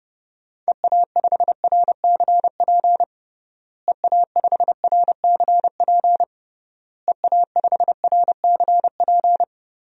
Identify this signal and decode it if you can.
Morse code, heard as a keyed tone, about 30 wpm, 700 Hz. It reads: EA5RCP EA5RCP EA5RCP
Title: DXCC-100 Call Signs - 6 Characters - Repeated 3x - 30wpm